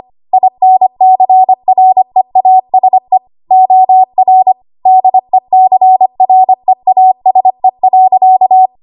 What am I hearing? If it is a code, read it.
Morse code, heard as a keyed tone, about 25 wpm, 750 Hz. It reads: INCREASE OR DECREASE.
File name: Morse / 25WPM